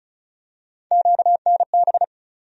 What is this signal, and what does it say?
Morse code, heard as a keyed tone, about 36 wpm, 700 Hz. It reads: QNB
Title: USA Call Sign Postfixes - 3 Letters 35wpm